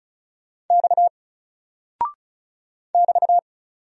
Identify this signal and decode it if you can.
Morse code, heard as a keyed tone, about 34 wpm, 700 Hz. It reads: X <BT>
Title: Single Character (Letter:Number:Punctuation:Prosign) 35wpm